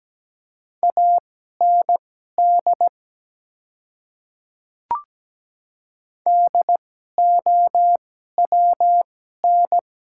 Morse code, heard as a keyed tone, about 17 wpm, spaced as for 12 wpm, 700 Hz. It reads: AND DOWN
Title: 2x Character Spacing - Top 100 Words - 17wpm